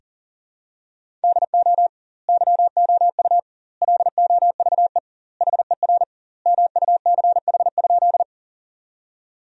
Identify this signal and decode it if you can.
Morse code, heard as a keyed tone, about 40 wpm, 700 Hz. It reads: DO YOU LOVE HER MUCH?
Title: 2x Word-Spacing - Sentences from Top 400 Words - 40wpm